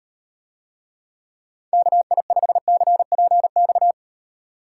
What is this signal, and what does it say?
Morse code, heard as a keyed tone, about 38 wpm, 700 Hz. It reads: KI5CPX